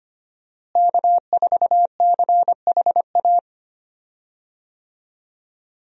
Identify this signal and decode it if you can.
Morse code, heard as a keyed tone, about 25 wpm, 700 Hz. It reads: K4CHA
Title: USA Call Signs - 5 Characters - Repeated 2x - 25wpm